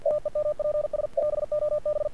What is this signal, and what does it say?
Morse code, heard as a keyed tone, about 48 wpm, 600 Hz, with radio noise. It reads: TEMPS BOB
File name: archived / CRSnet